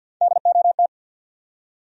Morse code, heard as a keyed tone, about 53 wpm, 700 Hz. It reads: DOT
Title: USA Common Names - Max 3 Letters - 50wpm